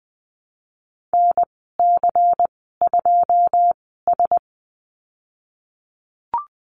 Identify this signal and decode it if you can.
Morse code, heard as a keyed tone, about 20 wpm, spaced as for 15 wpm, 700 Hz. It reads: NC2S
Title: 186 Lesson 30g - Letter b - Call Signs 20wpm